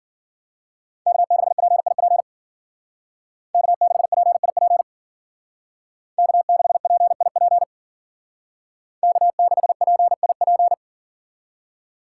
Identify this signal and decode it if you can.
Morse code, heard as a keyed tone, about 52 wpm, 700 Hz. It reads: K6PIP K6PIP K6PIP K6PIP